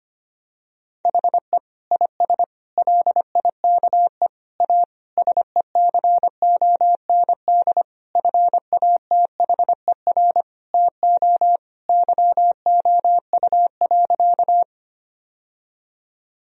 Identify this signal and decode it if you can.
Morse code, heard as a keyed tone, about 25 wpm, 700 Hz. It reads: HE IS LIKE A SECOND FATHER TO YOU.
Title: Sentences from Top 1000 Words - Review Part 1 - 25wpm